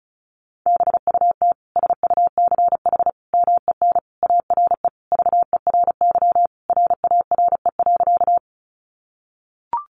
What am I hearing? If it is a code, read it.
Morse code, heard as a keyed tone, about 34 wpm, 700 Hz. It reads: BUT SUCH MEN ARE VERY RARE.
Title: Sentences from Top 1100 Words 35wpm